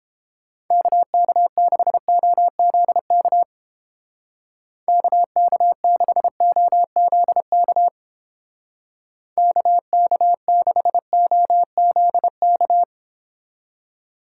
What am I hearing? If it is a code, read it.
Morse code, heard as a keyed tone, about 31 wpm, 700 Hz. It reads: KK6OZK KK6OZK KK6OZK